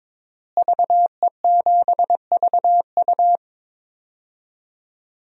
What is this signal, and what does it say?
Morse code, heard as a keyed tone, about 22 wpm, 700 Hz. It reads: VE7VU
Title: DXCC-100 Call Signs - 5 Characters - Repeated 3x - 22wpm